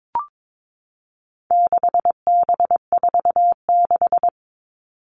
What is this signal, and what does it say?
Morse code, heard as a keyed tone, about 22 wpm, 700 Hz. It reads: 6B46
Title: All Cut Numbers - 4-Digit 22wpm